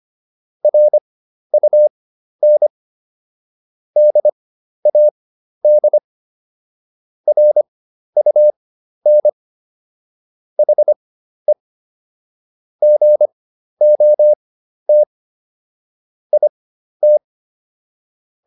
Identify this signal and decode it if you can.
Morse code, heard as a keyed tone, about 25 wpm, spaced as for 12 wpm, 600 Hz. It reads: RUN DAD RUN HE GOT IT